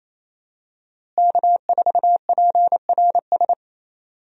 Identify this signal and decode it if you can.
Morse code, heard as a keyed tone, about 28 wpm, 700 Hz. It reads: K4PRS
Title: USA Call Signs - 5 Characters - Repeated 3x - 28wpm